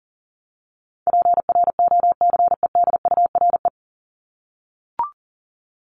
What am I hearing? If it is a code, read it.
Morse code, heard as a keyed tone, about 40 wpm, 700 Hz. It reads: PROCEDURE
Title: Top 1751-2000 Words 40wpm